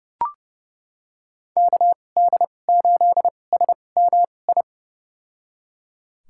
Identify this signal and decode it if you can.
Morse code, heard as a keyed tone, about 30 wpm, spaced as for 22 wpm, 700 Hz. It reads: KD8SMI
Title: USA Call Signs - 6 Characters - 2x Character Spacing - 30wpm